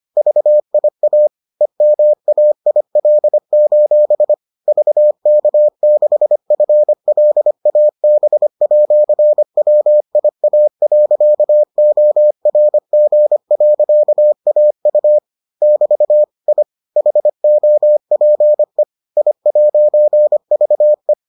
Morse code, heard as a keyed tone, about 25 wpm, 600 Hz. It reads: VIA EMAIL: VK6FLAB@WIA.ORG.AU <BT> I HOPE I'VE